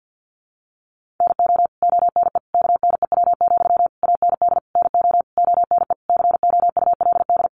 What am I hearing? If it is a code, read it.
Morse code, heard as a keyed tone, about 50 wpm, 700 Hz. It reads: NO ONE KNEW, AND NO ONE COULD
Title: Sentences from Top 1400 Words 50wpm